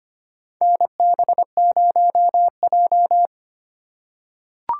Morse code, heard as a keyed tone, about 25 wpm, 700 Hz. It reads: NB0J
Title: USA Call Signs - 4 Characters - 2x Character Spacing - 25wpm